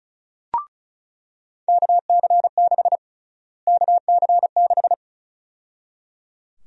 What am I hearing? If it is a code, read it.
Morse code, heard as a keyed tone, about 34 wpm, 700 Hz. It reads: KC6 KC6
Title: USA Call Sign Prefixes - Repeated 2x - 35wpm